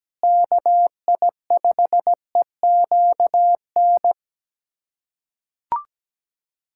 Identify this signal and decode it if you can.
Morse code, heard as a keyed tone, about 17 wpm, 700 Hz. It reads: KI5EQN
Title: USA Callsign DE Callsign - 6 Characters - 17wpm